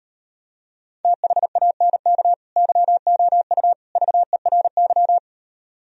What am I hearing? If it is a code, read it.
Morse code, heard as a keyed tone, about 38 wpm, 700 Hz. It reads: THANK YOU VERY